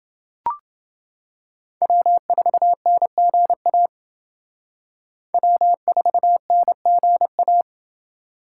Morse code, heard as a keyed tone, about 29 wpm, 700 Hz. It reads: W4NGA W4NGA